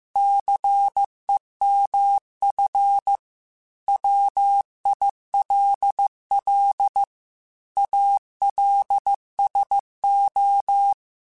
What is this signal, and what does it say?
Morse code, heard as a keyed tone, about 15 wpm, 800 Hz. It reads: CEMF WILL ALSO
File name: Technical Subjects / Technical Subjects-mp3s / XL @13